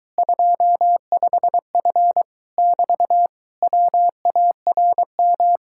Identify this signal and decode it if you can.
Morse code, heard as a keyed tone, about 22 wpm, 700 Hz. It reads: 25F <BT> WARM